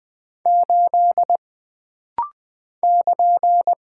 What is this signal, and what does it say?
Morse code, heard as a keyed tone, about 20 wpm, 700 Hz. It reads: 8 <KN>
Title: Single Character (Letter:Number:Punctuation:Prosign) - Rapid-Fire - 20wpm